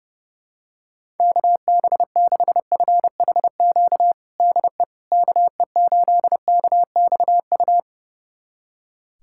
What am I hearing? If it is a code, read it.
Morse code, heard as a keyed tone, about 32 wpm, 700 Hz. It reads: KB6FHQ DE KE8KXU
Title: USA Callsign DE Callsign 30wpm